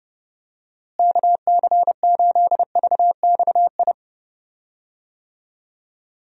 Morse code, heard as a keyed tone, about 30 wpm, 700 Hz. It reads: KC8VXI
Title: USA Call Signs - 6 Characters - Repeated 3x - 30wpm